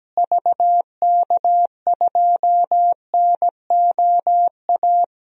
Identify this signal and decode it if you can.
Morse code, heard as a keyed tone, about 17 wpm, 700 Hz. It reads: VK2NOA
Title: DXCC-100 Call Signs - 6 Characters - Repeated 2x - 17wpm